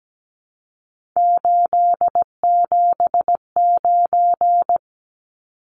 Morse code, heard as a keyed tone, about 17 wpm, 700 Hz. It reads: 879
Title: Numbers - 3-Digit 17wpm